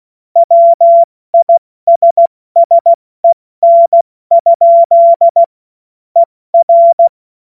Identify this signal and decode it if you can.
Morse code, heard as a keyed tone, about 16 wpm, 680 Hz. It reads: WISSEN? ER
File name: Morsekurs / Witze / Witz 01. November 2022 70 F80